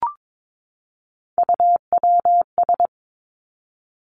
Morse code, heard as a keyed tone, about 22 wpm, 700 Hz. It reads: UWS
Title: Three Challenging Letter-Number Combinations 22wpm